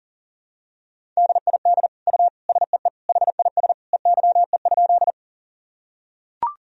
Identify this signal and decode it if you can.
Morse code, heard as a keyed tone, about 40 wpm, 700 Hz. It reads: DID U SEE HIS EYE?
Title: Sentences from Top 300 Words and Abbreviations 40wpm